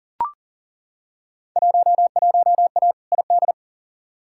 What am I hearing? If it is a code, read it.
Morse code, heard as a keyed tone, about 40 wpm, 700 Hz. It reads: 11A ID
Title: Field Day Exchanges Encore - 40wpm